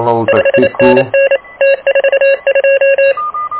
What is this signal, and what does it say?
Morse code, heard as a keyed tone, about 28 wpm, 2750 Hz, with noise. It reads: 5NN T42
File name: truSDX / KVPA ruSDXC